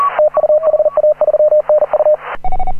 Morse code, heard as a keyed tone, about 38 wpm, 610 Hz. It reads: TU HA3NU
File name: AUDIO2018 / HA9A CW